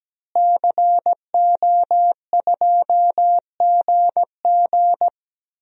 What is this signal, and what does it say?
Morse code, heard as a keyed tone, about 17 wpm, 700 Hz. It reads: CO2GG
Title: DXCC-100 Call Signs - 5 Characters - Repeated 3x - 17wpm